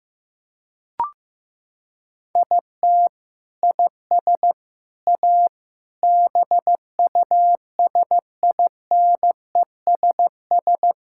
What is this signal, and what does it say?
Morse code, heard as a keyed tone, about 15 wpm, 700 Hz. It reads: IT IS A BUSINESS
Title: Sentences from Top 1600 Words 15wpm